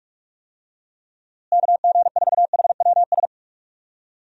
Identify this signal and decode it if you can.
Morse code, heard as a keyed tone, about 46 wpm, 700 Hz. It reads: KG4HWS
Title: USA Call Signs - 6 Characters - Repeated 2x - 45wpm